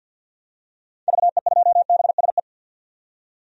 Morse code, heard as a keyed tone, about 51 wpm, 700 Hz. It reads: VE2BSE
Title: DXCC-100 Call Signs - 6 Characters - 50wpm